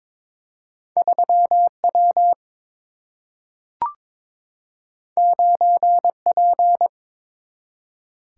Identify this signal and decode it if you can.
Morse code, heard as a keyed tone, about 21 wpm, 700 Hz. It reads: 3W 9P
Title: Single-Letter Single-Number 22wpm